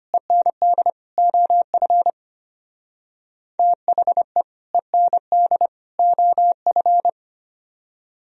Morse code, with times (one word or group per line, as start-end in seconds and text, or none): 0.1-0.9 END
1.2-2.1 OF
3.6-4.4 THE
4.7-5.7 END
6.0-7.1 OF